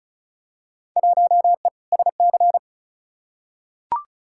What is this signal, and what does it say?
Morse code, heard as a keyed tone, about 34 wpm, 700 Hz. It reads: 1E SC
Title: Field Day Exchanges - 35wpm